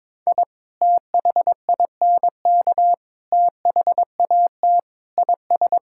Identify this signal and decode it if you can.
Morse code, heard as a keyed tone, about 22 wpm, 700 Hz. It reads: I THINK THAT IS
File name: practice / 22wpm / Sentences with 2-Word Phrases - 22wpm